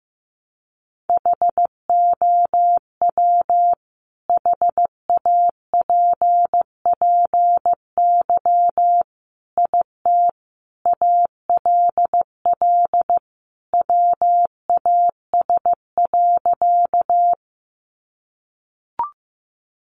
Morse code, heard as a keyed tone, about 15 wpm, 700 Hz. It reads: HOW HAPPY IT ALL WAS.
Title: Sentences from Top 1000 Words 15wpm